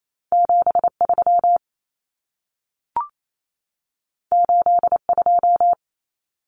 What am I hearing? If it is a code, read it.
Morse code, heard as a keyed tone, about 28 wpm, 700 Hz. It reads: 73 82
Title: Common Cut Numbers - 2-Digit 28wpm